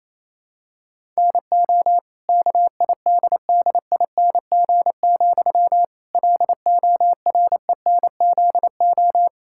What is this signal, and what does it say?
Morse code, heard as a keyed tone, about 28 wpm, 700 Hz. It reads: NO KIDDING, LORENZO
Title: Pangram Challenge v2 - 28wpm